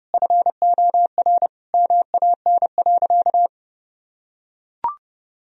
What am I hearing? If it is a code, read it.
Morse code, heard as a keyed tone, about 30 wpm, 700 Hz. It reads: FOR MAN.
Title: Sentences from Top 1000 Words - Review Part 3 - 30wpm